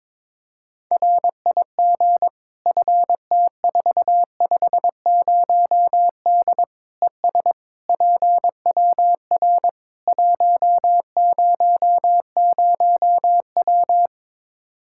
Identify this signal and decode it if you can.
Morse code, heard as a keyed tone, about 22 wpm, 700 Hz. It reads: RIG FT450D ES PWR 100W
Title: Practice QSO - Exchange 2 - Info - 22wpm